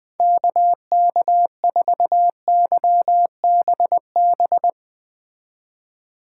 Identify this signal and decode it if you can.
Morse code, heard as a keyed tone, about 20 wpm, 700 Hz. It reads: KK4YBB
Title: Speed Racing - USA 2x3 Call Signs 17wpm